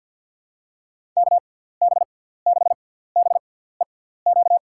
Morse code, heard as a keyed tone, about 53 wpm, spaced as for 17 wpm, 700 Hz. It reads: KB6BEQ
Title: USA Call Signs - 6 Characters - 6x Character Spacing - 50wpm